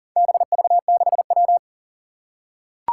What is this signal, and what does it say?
Morse code, heard as a keyed tone, about 38 wpm, 700 Hz. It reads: BV6W
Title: Four Challenging Letter-Number Combinations 40wpm